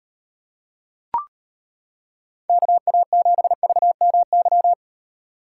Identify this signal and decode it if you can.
Morse code, heard as a keyed tone, about 32 wpm, 700 Hz. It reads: KA7VMY